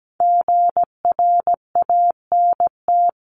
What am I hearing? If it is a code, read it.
Morse code, heard as a keyed tone, about 17 wpm, 700 Hz. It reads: GRANT
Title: CWT Exchange 17wpm